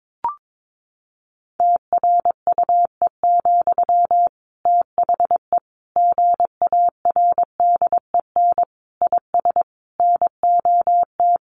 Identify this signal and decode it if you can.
Morse code, heard as a keyed tone, about 22 wpm, 700 Hz. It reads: TRUE, THE GARDEN IS NOT
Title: Sentences from Top 700 Words 22wpm